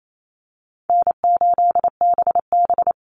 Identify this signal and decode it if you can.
Morse code, heard as a keyed tone, about 28 wpm, 700 Hz. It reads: N8BB
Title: NA SOTA Chaser Callsigns 28wpm